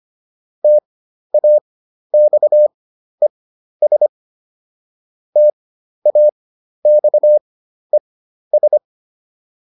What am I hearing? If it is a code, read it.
Morse code, heard as a keyed tone, about 25 wpm, spaced as for 12 wpm, 600 Hz. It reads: TAXES TAXES